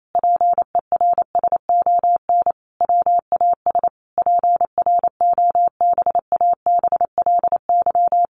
Morse code, heard as a keyed tone, about 28 wpm, 700 Hz. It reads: PERSON WAS PROBABLY
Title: Sentences from Top 1400 Words 28wpm